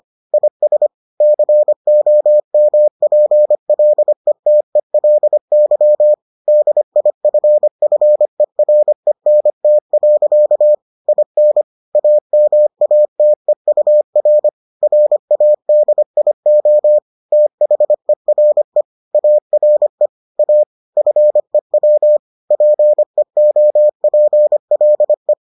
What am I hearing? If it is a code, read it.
Morse code, heard as a keyed tone, about 25 wpm, 600 Hz. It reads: IS COMPLETELY DIFFERENT. IN AMATEUR RADIO THERE ARE A FEW PEOPLE